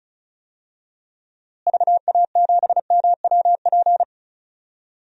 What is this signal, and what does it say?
Morse code, heard as a keyed tone, about 35 wpm, 700 Hz. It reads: VA7MWP
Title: DXCC-100 Call Signs - 6 Characters - Repeated 2x - 35wpm